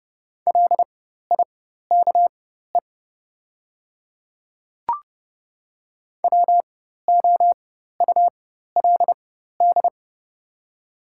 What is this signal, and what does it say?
Morse code, heard as a keyed tone, about 30 wpm, spaced as for 14 wpm, 700 Hz. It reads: LIKE WOULD
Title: 4x Character Spacing - Top 100 Words - 30wpm